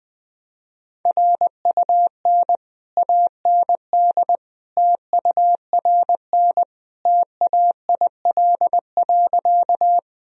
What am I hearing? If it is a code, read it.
Morse code, heard as a keyed tone, about 21 wpm, 700 Hz. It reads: RUN AND TURN TAIL.
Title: Sentences from Top 1000 Words - Review Part 3 - 20wpm